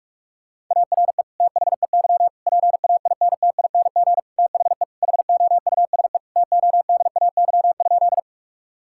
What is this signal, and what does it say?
Morse code, heard as a keyed tone, about 47 wpm, 700 Hz. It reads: ARE THEY PAINTING THE HOUSE TODAY?
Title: Sentences from Top 1000 Words - Review Part 6 - 45wpm